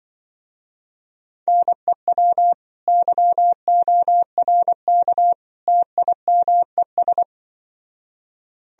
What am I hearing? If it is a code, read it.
Morse code, heard as a keyed tone, about 24 wpm, 700 Hz. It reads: NEW YORK TIMES
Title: Speed-Racing - Sets of 3 Words 20wpm